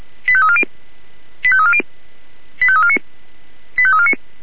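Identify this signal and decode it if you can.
Five-tone selective call (ZVEI), heard as quick runs of five tones, tones of about 70 ms. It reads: 85340 85340 85349 85349
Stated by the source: Received on SDR, ZVEI five-tone selective call 85340.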